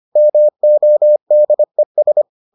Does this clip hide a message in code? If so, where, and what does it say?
Yes: Morse code from 0.2 to 2.2 s, heard as a keyed tone, about 25 wpm, 600 Hz. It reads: MODES